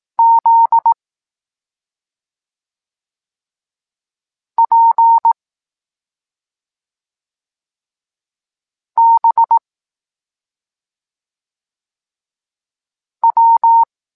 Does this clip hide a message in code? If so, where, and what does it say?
Yes: Morse code from 0.2 to 13.8 s, heard as a keyed tone, about 18 wpm, 920 Hz. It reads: Z P B W